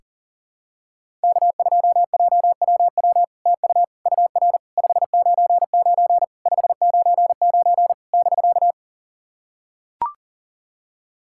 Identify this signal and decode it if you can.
Morse code, heard as a keyed tone, about 40 wpm, 700 Hz. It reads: K2JWW TU UR 599 599 <BK>
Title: SOTA QSO Practice 40wpm